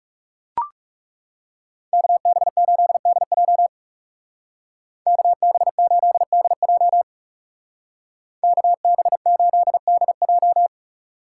Morse code, heard as a keyed tone, about 36 wpm, 700 Hz. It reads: KB8DJ KB8DJ KB8DJ
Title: Speed Racing - USA 2x2 Call Signs 30wpm